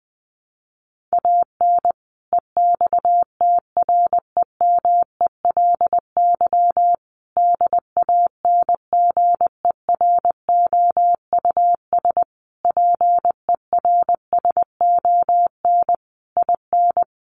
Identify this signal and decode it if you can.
Morse code, heard as a keyed tone, about 20 wpm, 700 Hz. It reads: AN EXTREMELY DANGEROUS PERSON IN
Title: Sentences from Top 1200 Words 20wpm